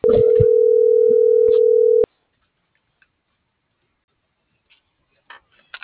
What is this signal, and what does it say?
Ringback tone (North America), 440 + 480 Hz.